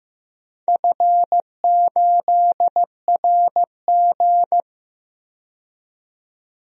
Morse code, heard as a keyed tone, about 16 wpm, 700 Hz. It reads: F8RG
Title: DXCC-100 Call Signs - 4 Characters - 15wpm